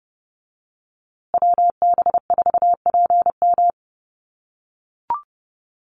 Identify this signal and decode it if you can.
Morse code, heard as a keyed tone, about 30 wpm, 700 Hz. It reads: WB4PM
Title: USA Call Signs Repeated 2x 30wpm